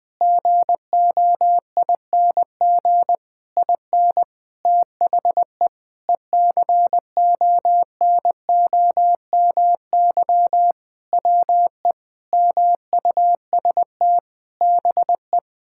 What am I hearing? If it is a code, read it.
Morse code, heard as a keyed tone, about 19 wpm, 700 Hz. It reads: GOING IN THE ECONOMY WE MUST BE
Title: Sentences with 8-Word Phrases - 20wpm